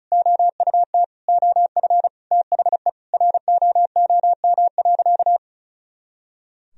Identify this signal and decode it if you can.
Morse code, heard as a keyed tone, about 36 wpm, 700 Hz. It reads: OUT OF THE ROOM.
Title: Sentences from Top 500 Words and Abbreviations 35wpm